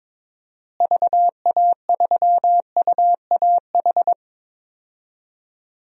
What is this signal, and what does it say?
Morse code, heard as a keyed tone, about 21 wpm, 700 Hz. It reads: VA3UAH
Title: DXCC-100 Call Signs - 6 Characters - Repeated 3x - 22wpm